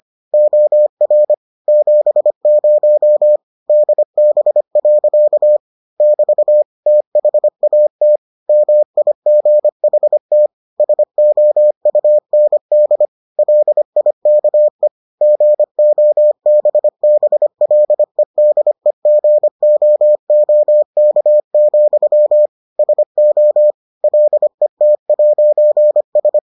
Morse code, heard as a keyed tone, about 25 wpm, 600 Hz. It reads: OR 70 DB. <BT> THAT MIGHT SOUND LIKE GOBBLEDEGOOK, SO LET'S